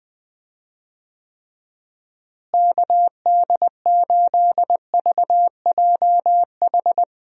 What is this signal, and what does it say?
Morse code, heard as a keyed tone, about 20 wpm, 700 Hz. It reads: KD8VJH
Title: USA Call Signs - 6 Characters - Repeated 2x - 20wpm